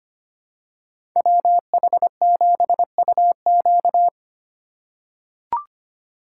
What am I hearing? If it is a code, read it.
Morse code, heard as a keyed tone, about 25 wpm, 700 Hz. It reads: WH7UQ
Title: USA Call Signs - 5 Characters - 2x Character Spacing - 25wpm